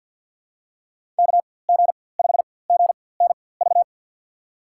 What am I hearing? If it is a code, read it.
Morse code, heard as a keyed tone, about 47 wpm, spaced as for 23 wpm, 700 Hz. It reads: KG5GNV
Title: USA Call Signs - 6 Characters - 4x Character Spacing - 50wpm